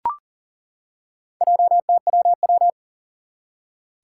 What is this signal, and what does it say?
Morse code, heard as a keyed tone, about 39 wpm, 700 Hz. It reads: JTWW
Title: Four Challenging Letter-Number Combinations 40wpm